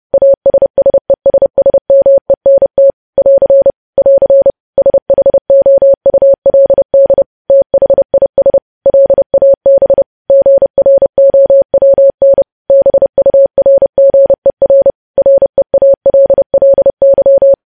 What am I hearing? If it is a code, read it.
Morse code, heard as a keyed tone, about 30 wpm, 550 Hz. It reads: ASSESSMENT <AR> <AR> SHOULD THIS LAB GROWN BURGER REALLY
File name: podcast / News headlines-1x-30x30